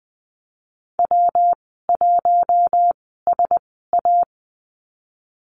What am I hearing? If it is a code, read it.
Morse code, heard as a keyed tone, about 20 wpm, spaced as for 14 wpm, 700 Hz. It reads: W1SA
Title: 122 Lesson 20g - Number 3 - Call Signs 20wpm